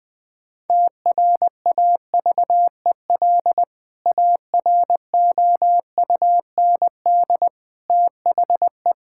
Morse code, heard as a keyed tone, about 21 wpm, 700 Hz. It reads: TRAVEL AROUND THE
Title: Sentences with 3-Word Phrases - 20wpm